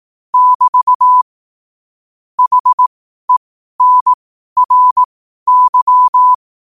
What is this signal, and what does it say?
Morse code, heard as a keyed tone, about 18 wpm, spaced as for 12 wpm, 1000 Hz. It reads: <BT> HENRY
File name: qotdfiles / 12wpm